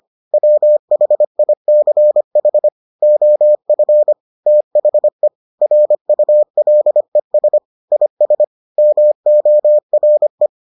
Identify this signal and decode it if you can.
Morse code, heard as a keyed tone, about 25 wpm, 600 Hz. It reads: WHICH OF THE RULES IS MORE